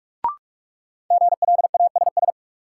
Morse code, heard as a keyed tone, about 41 wpm, 700 Hz. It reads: GLASS
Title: Sentences from Top 1300 Words - New Word Drill Alt - 45wpm